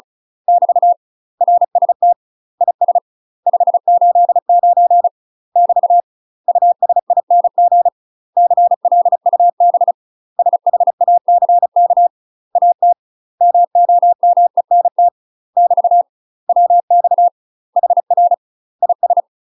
Morse code, heard as a keyed tone, about 34 wpm, 700 Hz. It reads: <BT> RST IS 589 <BT> USING CLUB SHACK AT MOMENT <BT> WX HR IS
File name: UK MASTER QSOs TRAINING/35WPM